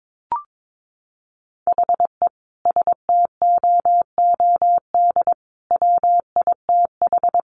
Morse code, heard as a keyed tone, about 22 wpm, 700 Hz. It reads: HE STOOD WITH
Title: Sentences from Top 1500 Words 22wpm